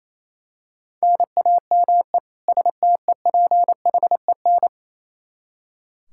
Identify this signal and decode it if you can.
Morse code, heard as a keyed tone, about 28 wpm, 700 Hz. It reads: NAME STEPHEN
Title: Practice QSO - Exchange 1 - Beginning - 28wpm